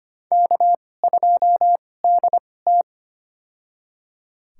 Morse code, heard as a keyed tone, about 25 wpm, spaced as for 18 wpm, 700 Hz. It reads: K2DT
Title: USA Call Signs - 4 Characters - 2x Character Spacing - 25wpm